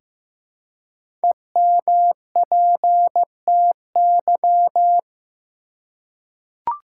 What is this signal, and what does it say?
Morse code, heard as a keyed tone, about 15 wpm, 700 Hz. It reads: EMPTY
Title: Five Letter Words - 15wpm